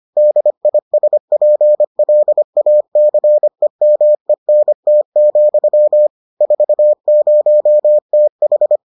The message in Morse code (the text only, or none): DISPLACEMENT, 40TH